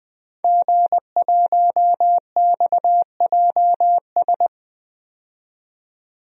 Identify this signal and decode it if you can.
Morse code, heard as a keyed tone, about 20 wpm, 700 Hz. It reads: G1XJS
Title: DXCC-100 Call Signs - 5 Characters - 20wpm